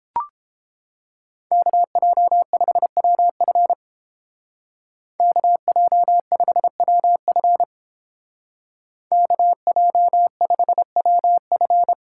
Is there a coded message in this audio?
Yes: Morse code from 1.5 to 11.9 s, heard as a keyed tone, about 30 wpm, 700 Hz. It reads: KJ5WF KJ5WF KJ5WF